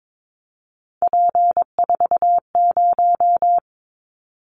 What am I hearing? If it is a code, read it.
Morse code, heard as a keyed tone, about 22 wpm, 700 Hz. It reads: P40
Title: Callsign Prefixes for Top 100 DXCC - 22wpm